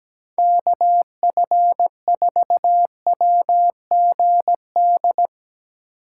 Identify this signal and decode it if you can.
Morse code, heard as a keyed tone, about 17 wpm, 700 Hz. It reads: KF4WGD
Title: Speed Racing - USA 2x3 Call Signs 17wpm